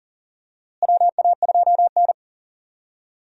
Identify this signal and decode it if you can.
Morse code, heard as a keyed tone, about 39 wpm, 700 Hz. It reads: WA2N
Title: USA Call Signs - 4 Characters - Repeated 2x - 40wpm